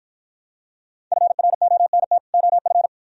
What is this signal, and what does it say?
Morse code, heard as a keyed tone, about 53 wpm, 700 Hz. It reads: FRONT OF